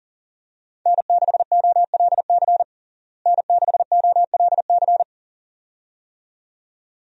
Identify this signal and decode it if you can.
Morse code, heard as a keyed tone, about 40 wpm, 700 Hz. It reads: N6OLC N6OLC
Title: USA Call Signs - 5 Characters - Repeated 2x - 40wpm